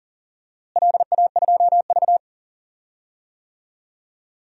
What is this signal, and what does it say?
Morse code, heard as a keyed tone, about 40 wpm, 700 Hz. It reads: LA2V
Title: DXCC-100 Call Signs - 4 Characters - Repeated 2x - 40wpm